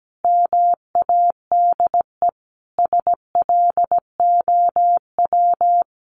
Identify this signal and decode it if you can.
Morse code, heard as a keyed tone, about 17 wpm, 700 Hz. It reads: MADE SLOW
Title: Sentences from Top 1400 Words 17wpm